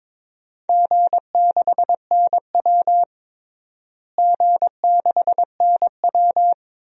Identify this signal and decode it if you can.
Morse code, heard as a keyed tone, about 21 wpm, 700 Hz. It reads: G6NW G6NW
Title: DXCC-100 Call Signs - 4 Characters - Repeated 2x - 22wpm